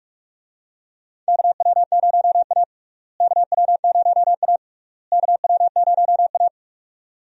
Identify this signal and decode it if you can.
Morse code, heard as a keyed tone, about 46 wpm, 700 Hz. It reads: KW0A KW0A KW0A
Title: USA Call Signs - 4 Characters - Repeated 3x - 45wpm